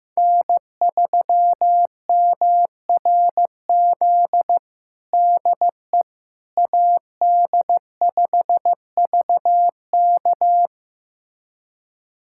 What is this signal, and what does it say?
Morse code, heard as a keyed tone, about 15 wpm, 700 Hz. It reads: N3MRZ DE AD5VK
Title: USA Callsign DE Callsign - 5 Characters - 15wpm